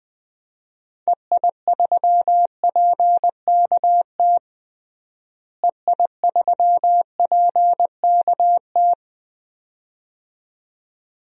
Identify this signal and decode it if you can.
Morse code, heard as a keyed tone, about 20 wpm, 700 Hz. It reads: EI3PKT EI3PKT
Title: DXCC-100 Call Signs - 6 Characters - Repeated 2x - 20wpm